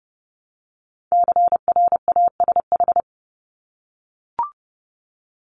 Morse code, heard as a keyed tone, about 30 wpm, 700 Hz. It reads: CRASH
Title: Top 1751-2000 Words 30wpm